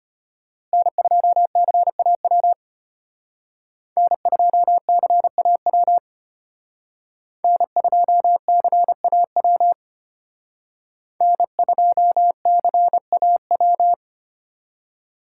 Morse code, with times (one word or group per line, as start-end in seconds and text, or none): 0.7-2.5 N2CAW
4.0-6.0 N2CAW
7.4-9.7 N2CAW
11.2-13.9 N2CAW